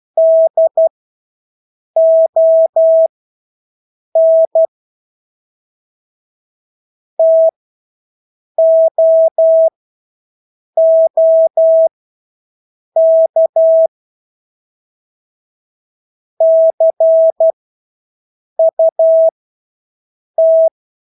Morse code, heard as a keyed tone, about 12 wpm, spaced as for 6 wpm, 650 Hz. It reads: DON TOOK CUT
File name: Engelske ord 12-6 wpm